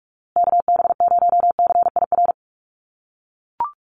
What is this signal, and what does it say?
Morse code, heard as a keyed tone, about 44 wpm, 700 Hz. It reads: KB0CIR
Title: USA Call Signs Repeated 3x 45wpm